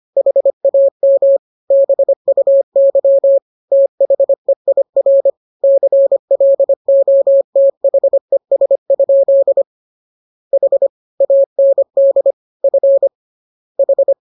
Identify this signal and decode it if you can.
Morse code, heard as a keyed tone, about 25 wpm, 550 Hz. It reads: HAM BUY THEIR CLOTHES? H AND F H
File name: jokes / joke 19 25 0000